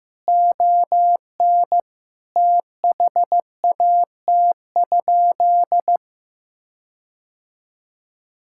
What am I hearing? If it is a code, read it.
Morse code, heard as a keyed tone, about 15 wpm, 700 Hz. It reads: ON THAT?
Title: Sentences from Top 1600 Words 15wpm